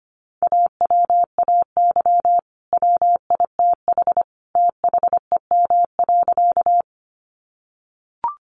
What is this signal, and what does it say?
Morse code, heard as a keyed tone, about 25 wpm, 700 Hz. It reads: AWAY WITH THEM.
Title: Sentences from Top 600 Words 25wpm